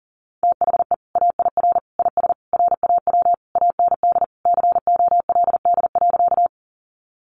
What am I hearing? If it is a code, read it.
Morse code, heard as a keyed tone, about 40 wpm, 700 Hz. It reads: THE AIR IS RAW AND COLD.
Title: Sentences from Top 1500 Words 40wpm